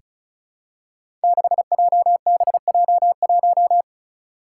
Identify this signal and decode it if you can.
Morse code, heard as a keyed tone, about 36 wpm, 700 Hz. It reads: 6JBJ1
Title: Five Challenging Letter-Number Combinations 35wpm